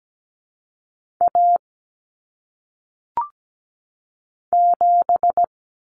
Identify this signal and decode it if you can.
Morse code, heard as a keyed tone, about 17 wpm, 700 Hz. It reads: A 7
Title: Common Cut Numbers - 1-Digit 17wpm